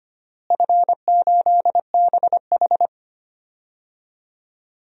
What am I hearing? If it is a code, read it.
Morse code, heard as a keyed tone, about 25 wpm, 700 Hz. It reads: F8BH
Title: DXCC-100 Call Signs - 4 Characters - Repeated 3x - 25wpm